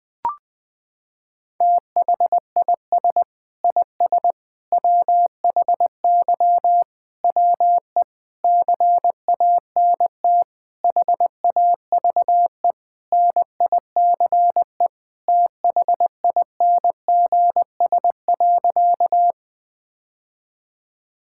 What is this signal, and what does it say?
Morse code, heard as a keyed tone, about 19 wpm, 700 Hz. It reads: THIS IS WHY WE CANT HAVE NICE THINGS.
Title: Sentences with 8-Word Phrases - 20wpm